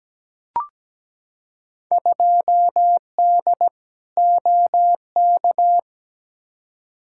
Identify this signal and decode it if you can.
Morse code, heard as a keyed tone, about 17 wpm, 700 Hz. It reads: 2D OK
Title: Field Day Exchanges Encore - 17wpm